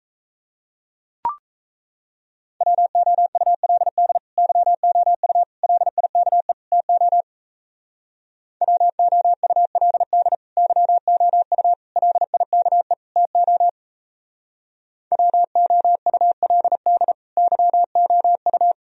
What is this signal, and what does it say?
Morse code, heard as a keyed tone, about 36 wpm, 700 Hz. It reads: WOULD YOU LIKE TO WOULD YOU LIKE TO WOULD YOU